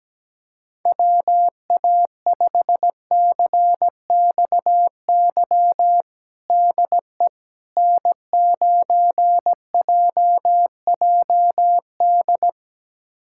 Morse code, heard as a keyed tone, about 17 wpm, 700 Hz. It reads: WA5CXY DE N9JJD
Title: USA Callsign DE Callsign 17wpm